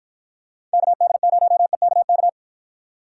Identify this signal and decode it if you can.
Morse code, heard as a keyed tone, about 53 wpm, 700 Hz. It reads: KD0EKK